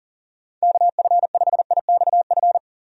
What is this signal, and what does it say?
Morse code, heard as a keyed tone, about 40 wpm, 700 Hz. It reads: KF5IXF
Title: USA Call Signs - 6 Characters - 40wpm